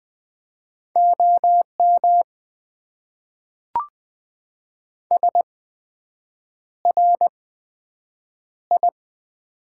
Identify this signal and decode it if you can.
Morse code, heard as a keyed tone, about 20 wpm, spaced as for 7 wpm, 700 Hz. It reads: OM SRI
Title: Top 90 QSO Elements - 8x Character Spacing - 20wpm